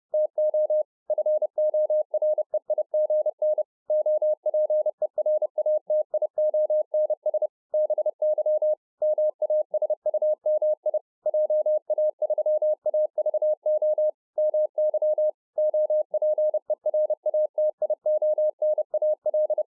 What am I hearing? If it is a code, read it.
Morse code, heard as a keyed tone, about 30 wpm, 610 Hz. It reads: TO FOREIGN OPERATIONS BY MASUMI JA3AVO MY OPERATIONAL